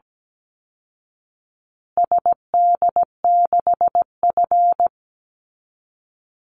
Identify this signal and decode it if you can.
Morse code, heard as a keyed tone, about 17 wpm, 700 Hz. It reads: SD6F
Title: Call Signs Repeated 2x - 17wpm